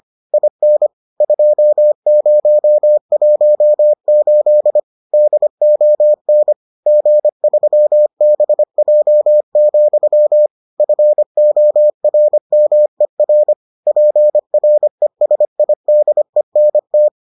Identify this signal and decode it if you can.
Morse code, heard as a keyed tone, about 25 wpm, 600 Hz. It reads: IN 2018 DON G3BJ, FORMER PRESIDENT